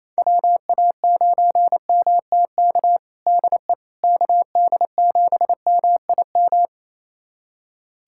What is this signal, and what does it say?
Morse code, heard as a keyed tone, about 28 wpm, 700 Hz. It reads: WA9MTK DE KD7MIM
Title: USA Callsign DE Callsign - 6 Characters - 28wpm